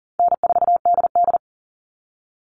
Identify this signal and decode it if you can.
Morse code, heard as a keyed tone, about 40 wpm, 700 Hz. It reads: N4DD
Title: All Cut Numbers - 4-Digit 40wpm